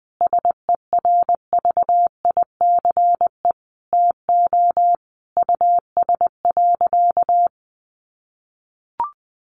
Morse code, heard as a keyed tone, about 20 wpm, 700 Hz. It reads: SERVICE TO US.
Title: Sentences from Top 1200 Words 20wpm